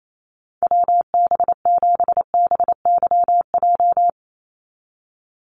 Morse code, heard as a keyed tone, about 28 wpm, 700 Hz. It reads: WB7BYJ
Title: USA Call Signs 28wpm